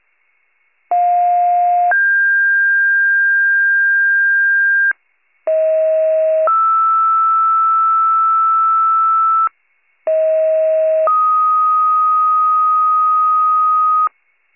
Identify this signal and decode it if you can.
Two-tone page, heard as a one-second tone, then a three-second tone, about 710 then 1600 Hz, about 640 then 1320 Hz, about 640 then 1250 Hz. It is heard from 0.9 to 14.1 s.